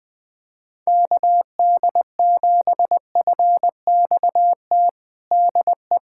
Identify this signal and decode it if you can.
Morse code, heard as a keyed tone, about 20 wpm, 700 Hz. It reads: KD7FXT DE
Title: USA Callsign DE Callsign - 6 Characters - 20wpm